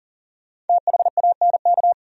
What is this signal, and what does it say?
Morse code, heard as a keyed tone, about 52 wpm, 700 Hz. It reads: THANK